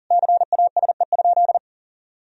Morse code, heard as a keyed tone, about 40 wpm, 700 Hz. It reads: CASE?
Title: Sentences from Top 1000 Words - Review Part 4 - 40wpm